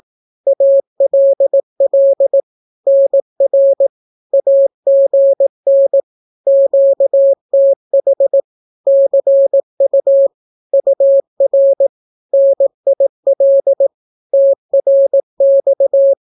Morse code, heard as a keyed tone, about 18 wpm, 550 Hz. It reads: ALL NR AGN QTH CU UR NIL TRX
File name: naukapodstaw / skroty / skroty